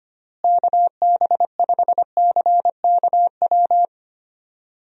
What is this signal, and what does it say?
Morse code, heard as a keyed tone, about 25 wpm, 700 Hz. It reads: KB5CKW
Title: USA Call Signs - 6 Characters - Repeated 2x - 25wpm